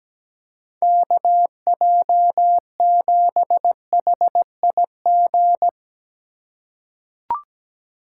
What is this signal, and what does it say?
Morse code, heard as a keyed tone, about 17 wpm, 700 Hz. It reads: KJ7HIG
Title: USA Call Signs - 6 Characters - 3x Character Spacing - 17wpm